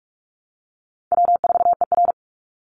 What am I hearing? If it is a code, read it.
Morse code, heard as a keyed tone, about 44 wpm, 700 Hz. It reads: W4ER
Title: Call Signs Repeated 3x - 45wpm